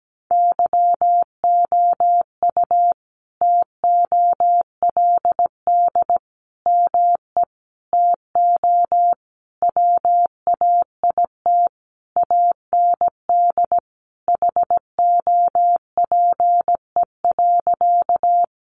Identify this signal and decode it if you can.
Morse code, heard as a keyed tone, about 17 wpm, 700 Hz. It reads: YOU TOLD ME TO WAIT AND HOPE.
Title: Sentences from Top 1000 Words 17wpm